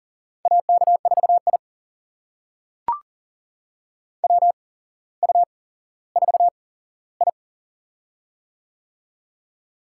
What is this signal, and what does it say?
Morse code, heard as a keyed tone, about 39 wpm, spaced as for 17 wpm, 700 Hz. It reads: AK4I WU4I
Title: USA Call Signs - 4 Characters - 8x Character Spacing - 40wpm